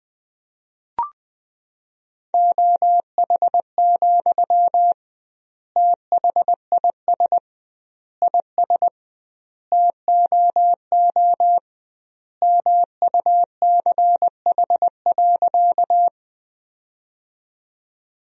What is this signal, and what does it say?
Morse code, heard as a keyed tone, about 20 wpm, 700 Hz. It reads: OH, THIS IS TOO MUCH.
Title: 2x Word-Spacing - Sentences from Top 500 Words - 20wpm